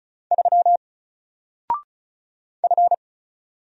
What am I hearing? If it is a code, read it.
Morse code, heard as a keyed tone, about 36 wpm, 700 Hz. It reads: 3 F
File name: Single-Letter-Number-35wpm